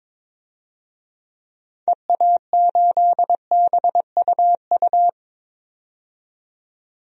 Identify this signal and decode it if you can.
Morse code, heard as a keyed tone, about 22 wpm, 700 Hz. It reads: EA8BUU
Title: DXCC-100 Call Signs - 6 Characters - 22wpm